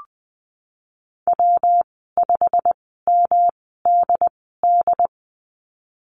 Morse code, heard as a keyed tone, about 20 wpm, spaced as for 15 wpm, 700 Hz. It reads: W5MDD
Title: 122 Lesson 20g - Number 3 - Call Signs 20wpm